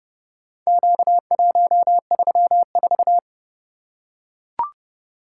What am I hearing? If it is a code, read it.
Morse code, heard as a keyed tone, about 27 wpm, 700 Hz. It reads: Q134